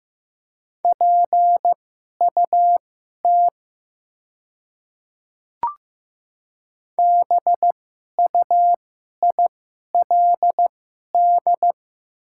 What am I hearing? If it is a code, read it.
Morse code, heard as a keyed tone, about 15 wpm, spaced as for 11 wpm, 700 Hz. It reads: PUT BUILD
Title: 2x Character Spacing - Top 300 Words - 15wpm